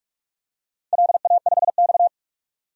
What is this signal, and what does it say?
Morse code, heard as a keyed tone, about 46 wpm, 700 Hz. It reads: LA5X
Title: DXCC-100 Call Signs - 4 Characters - 45wpm